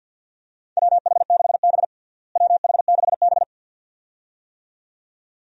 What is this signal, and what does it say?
Morse code, heard as a keyed tone, about 52 wpm, 700 Hz. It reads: WH6B WH6B
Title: USA Call Signs - 4 Characters - Repeated 2x - 50wpm